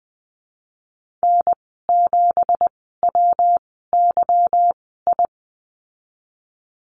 Morse code, heard as a keyed tone, about 20 wpm, spaced as for 15 wpm, 700 Hz. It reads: N7WYI
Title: 167 Lesson 27g - Number 7 - Call Signs 20wpm